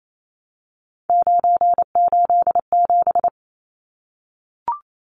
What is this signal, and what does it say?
Morse code, heard as a keyed tone, about 28 wpm, 700 Hz. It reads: 987
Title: Numbers - 3-Digit 28wpm